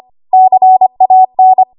Morse code, heard as a keyed tone, about 25 wpm, 750 Hz. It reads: CAN
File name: Archive / 25 WPM / 25WPM